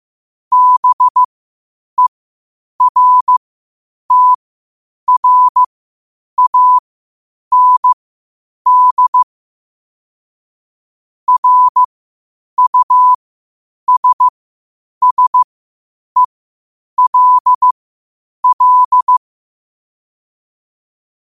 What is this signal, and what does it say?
Morse code, heard as a keyed tone, about 15 wpm, spaced as for 8 wpm, 1000 Hz. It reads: BERTRAND RUSSELL
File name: qotdfiles / 8wpm